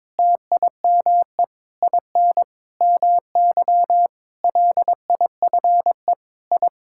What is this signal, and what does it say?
Morse code, heard as a keyed tone, about 22 wpm, 700 Hz. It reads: TIME IN MY LIFE I
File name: practice / 22wpm / Sets of 8 Words - 22wpm